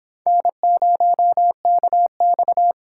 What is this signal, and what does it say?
Morse code, heard as a keyed tone, about 23 wpm, 700 Hz. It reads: N0KX